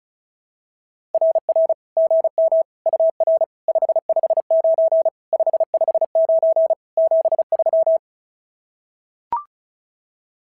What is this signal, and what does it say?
Morse code, heard as a keyed tone, about 33 wpm, 660 Hz. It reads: RR GM UR 559 559 73
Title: SOTA QSO Practice 35wpm